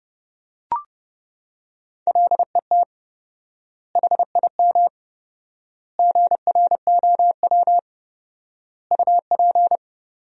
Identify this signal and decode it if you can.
Morse code, heard as a keyed tone, about 30 wpm, 700 Hz. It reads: LET HIM GROW UP
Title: 4x Word-Spacing - Sentences from Top 300 Words - 30wpm